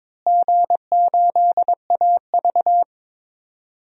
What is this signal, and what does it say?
Morse code, heard as a keyed tone, about 22 wpm, 700 Hz. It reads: G8AV
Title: DXCC-100 Call Signs - 4 Characters - Repeated 3x - 22wpm